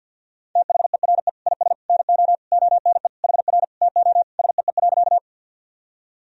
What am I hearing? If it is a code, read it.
Morse code, heard as a keyed tone, about 52 wpm, 700 Hz. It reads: THERE IS NO ONE HR TO SEE.
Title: Sentences from Top 200 Words and Abbreviations 50wpm